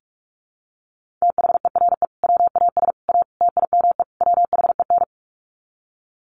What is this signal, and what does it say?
Morse code, heard as a keyed tone, about 44 wpm, 700 Hz. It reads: THERE WAS A TIME WHEN
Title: Sets of 5 Words 45wpm